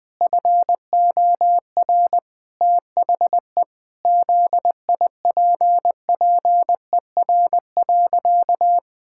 Morse code, heard as a keyed tone, about 20 wpm, 700 Hz. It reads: FOR THE ZIPPER.
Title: Pangram Challenge v2 - 20wpm